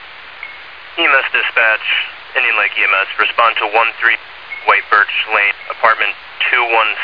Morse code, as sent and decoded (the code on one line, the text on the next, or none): none